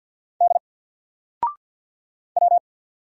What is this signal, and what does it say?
Morse code, heard as a keyed tone, about 48 wpm, 700 Hz. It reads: D W
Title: Single Character (Letter:Number:Punctuation:Prosign) 50wpm